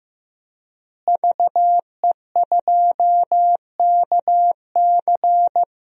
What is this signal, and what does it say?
Morse code, heard as a keyed tone, about 15 wpm, 700 Hz. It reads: VE2KC
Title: DXCC-100 Call Signs - 5 Characters - Repeated 3x - 15wpm